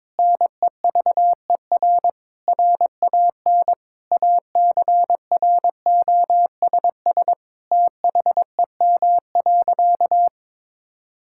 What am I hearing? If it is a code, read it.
Morse code, heard as a keyed tone, about 21 wpm, 700 Hz. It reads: NEVER RAN ACROSS THEM.
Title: Sentences from Top 1000 Words - Review Part 5 - 22wpm